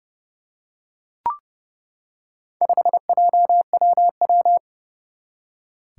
Morse code, heard as a keyed tone, about 30 wpm, 700 Hz. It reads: 5JWW
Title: Four Challenging Letter-Number Combinations 30wpm